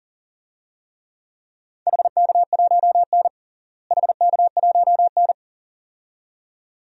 Morse code, heard as a keyed tone, about 40 wpm, 700 Hz. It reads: HK1N HK1N
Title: DXCC-100 Call Signs - 4 Characters - Repeated 2x - 40wpm